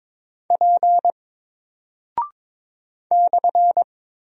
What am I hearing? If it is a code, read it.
Morse code, heard as a keyed tone, about 21 wpm, 700 Hz. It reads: P /
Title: Single Character (Letter:Number:Punctuation:Prosign) Alt 22wpm